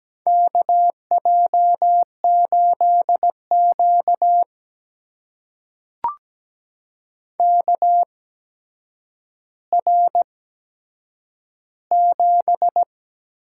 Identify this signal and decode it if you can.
Morse code, heard as a keyed tone, about 17 wpm, spaced as for 8 wpm, 700 Hz. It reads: KJ8Q KR7
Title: USA Call Signs - 4 Characters - 8x Character Spacing - 17wpm